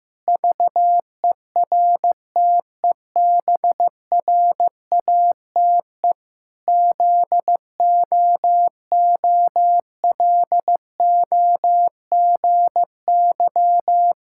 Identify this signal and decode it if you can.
Morse code, heard as a keyed tone, about 15 wpm, 700 Hz. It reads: VERTEBRATE ZOOLOGY
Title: Pangram Challenge v2 - 15wpm